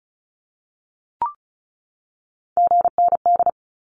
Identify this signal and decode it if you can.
Morse code, heard as a keyed tone, about 34 wpm, 700 Hz. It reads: GND
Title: Abbreviations 35wpm